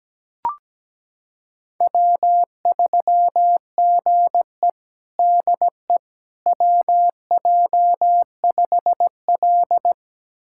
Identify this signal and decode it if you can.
Morse code, heard as a keyed tone, about 17 wpm, 700 Hz. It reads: W3GE DE WJ5L
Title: USA Callsign DE Callsign - 4 Characters - 17wpm